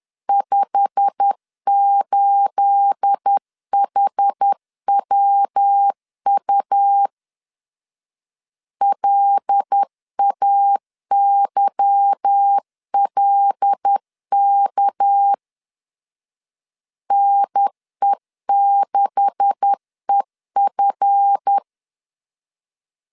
Morse code, heard as a keyed tone, about 10 wpm, 800 Hz. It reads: 58HWU LAYLK NE6EF
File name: CW / DET / 1 9 1500C S B 10wpm 4439